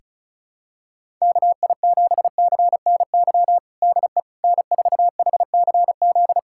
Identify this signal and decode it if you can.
Morse code, heard as a keyed tone, about 34 wpm, 700 Hz. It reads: KI7CNY DE N4HCZ
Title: USA Callsign DE Callsign 35wpm